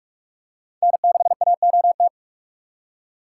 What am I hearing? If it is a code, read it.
Morse code, heard as a keyed tone, about 45 wpm, 700 Hz. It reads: N6AOT
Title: USA Call Signs - 5 Characters - 3x Character Spacing - 45wpm